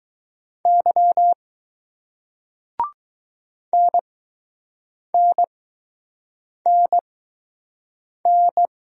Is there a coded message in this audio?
Yes: Morse code from 0.7 to 8.7 s, heard as a keyed tone, about 20 wpm, spaced as for 6 wpm, 700 Hz. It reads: Y NNNN